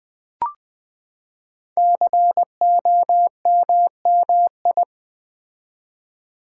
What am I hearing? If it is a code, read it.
Morse code, heard as a keyed tone, about 20 wpm, 700 Hz. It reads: COMMI
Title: Build A Word - 5 Letters At A Time - 12-Letter Words - 20wpm